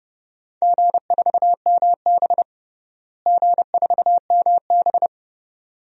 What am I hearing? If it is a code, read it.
Morse code, heard as a keyed tone, about 32 wpm, 700 Hz. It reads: G4MB G4MB
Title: DXCC-100 Call Signs - 4 Characters - Repeated 2x - 30wpm